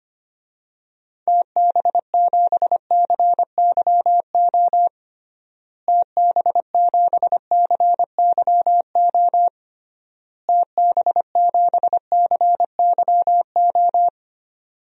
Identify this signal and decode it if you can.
Morse code, heard as a keyed tone, about 25 wpm, 700 Hz. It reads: TB7CYO TB7CYO TB7CYO
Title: DXCC-100 Call Signs - 6 Characters - Repeated 3x - 25wpm